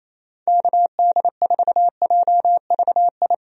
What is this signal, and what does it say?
Morse code, heard as a keyed tone, about 28 wpm, 700 Hz. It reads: KD4JVI
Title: USA Call Signs - 6 Characters - 28wpm